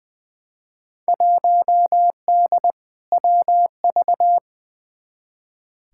Morse code, heard as a keyed tone, about 20 wpm, 700 Hz. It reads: 1D WV
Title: Field Day Exchanges - 20wpm